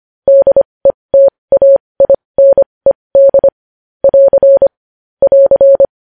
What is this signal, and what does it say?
Morse code, heard as a keyed tone, about 25 wpm, spaced as for 20 wpm, 550 Hz. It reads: DETAINED <AR> <AR>
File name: podcast / News headlines-1x-25x20